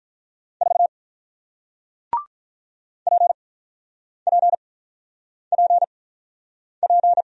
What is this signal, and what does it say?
Morse code, heard as a keyed tone, about 48 wpm, spaced as for 9 wpm, 700 Hz. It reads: 4 PPPP